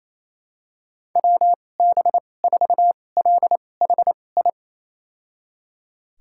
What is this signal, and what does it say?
Morse code, heard as a keyed tone, about 28 wpm, spaced as for 20 wpm, 700 Hz. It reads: WB4LHI
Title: USA Call Signs - 6 Characters - 2x Character Spacing - 28wpm